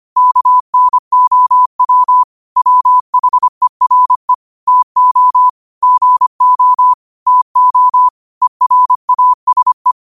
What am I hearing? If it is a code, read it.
Morse code, heard as a keyed tone, about 25 wpm, 1000 Hz. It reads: KNOW WHERE TO GO TO ERASE
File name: qotdfiles / 25wpm